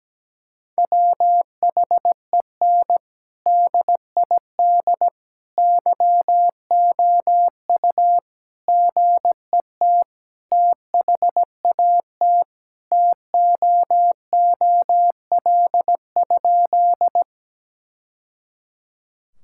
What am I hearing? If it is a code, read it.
Morse code, heard as a keyed tone, about 17 wpm, 700 Hz. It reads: WHEN DID YOU GET THAT TOOL?
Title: Sentences from Top 1000 Words - Review Part 6 - 17wpm